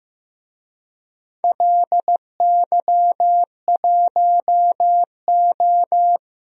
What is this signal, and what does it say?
Morse code, heard as a keyed tone, about 15 wpm, 700 Hz. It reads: LY1O
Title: DXCC-100 Call Signs - 4 Characters - Repeated 3x - 15wpm